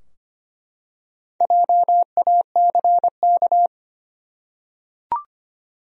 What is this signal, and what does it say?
Morse code, heard as a keyed tone, about 25 wpm, 700 Hz. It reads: JACK
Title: USA Common Names - Max 5 Letters - 25wpm